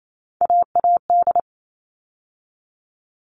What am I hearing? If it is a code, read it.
Morse code, heard as a keyed tone, about 28 wpm, 700 Hz. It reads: AAD
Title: All Cut Numbers - 3-Digit 28wpm